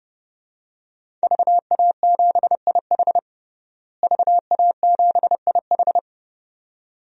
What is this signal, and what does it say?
Morse code, heard as a keyed tone, about 32 wpm, 700 Hz. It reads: VA7IH VA7IH
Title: DXCC-100 Call Signs - 5 Characters - Repeated 2x - 30wpm